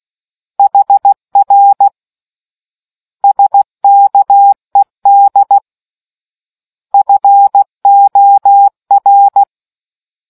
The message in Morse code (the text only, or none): HR SKED FOR